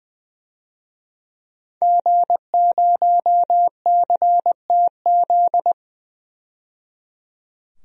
Morse code, heard as a keyed tone, about 20 wpm, 700 Hz. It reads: G0CTZ
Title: DXCC-100 Call Signs - 5 Characters - 20wpm